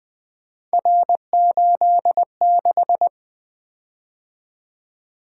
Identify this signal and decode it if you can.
Morse code, heard as a keyed tone, about 20 wpm, 700 Hz. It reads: R86
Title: Three Letter-Number Combinations - 20wpm